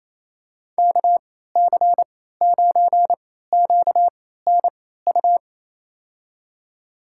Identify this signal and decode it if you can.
Morse code, heard as a keyed tone, about 27 wpm, spaced as for 16 wpm, 700 Hz. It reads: KC9QNU
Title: USA Call Signs - 6 Characters - 3x Character Spacing - 28wpm